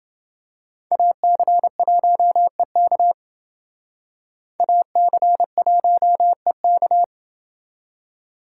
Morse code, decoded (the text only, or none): AC1EK AC1EK